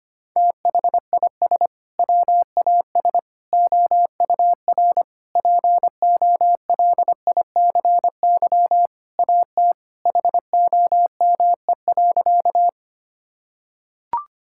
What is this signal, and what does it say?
Morse code, heard as a keyed tone, about 25 wpm, 700 Hz. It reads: THIS WAS OUR POLICY AT HOME.
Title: Sentences from Top 1000 Words - Review Part 5 - 25wpm